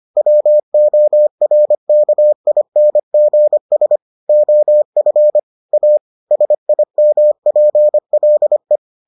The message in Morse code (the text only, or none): WORKINGS OF A SIMPLE